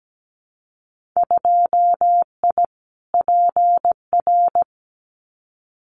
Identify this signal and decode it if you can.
Morse code, heard as a keyed tone, about 17 wpm, 700 Hz. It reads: 2I PR
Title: Winter Field Day Exchanges Encore 17wpm